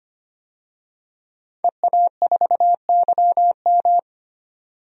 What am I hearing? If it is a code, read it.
Morse code, heard as a keyed tone, about 25 wpm, 700 Hz. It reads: EA4YM
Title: DXCC-100 Call Signs - 5 Characters - 25wpm